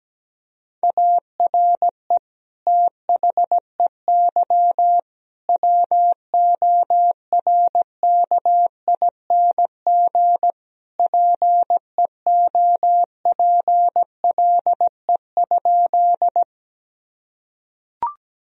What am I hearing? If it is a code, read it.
Morse code, heard as a keyed tone, about 17 wpm, 700 Hz. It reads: ARE THEY WORKING PEOPLE?
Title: Sentences from Top 1000 Words - Review Part 4 - 17wpm